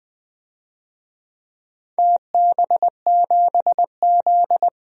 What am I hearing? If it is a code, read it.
Morse code, heard as a keyed tone, about 20 wpm, 700 Hz. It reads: TB7Z
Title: DXCC-100 Call Signs - 4 Characters - 20wpm